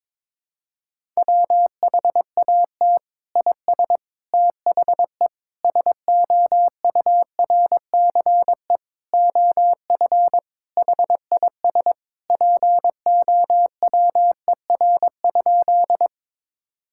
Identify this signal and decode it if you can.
Morse code, heard as a keyed tone, about 22 wpm, 700 Hz. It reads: WHAT IS THE SOURCE OF HIS POWER?
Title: Sentences from Top 1000 Words - Review Part 5 - 22wpm